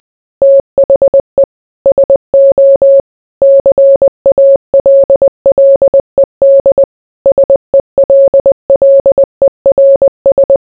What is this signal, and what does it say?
Morse code, heard as a keyed tone, about 20 wpm, 550 Hz. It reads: THE SO CALLED SELLERS